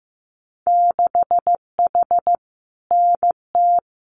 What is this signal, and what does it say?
Morse code, heard as a keyed tone, about 15 wpm, 700 Hz. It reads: 6H NT
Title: Winter Field Day Exchanges Encore 15wpm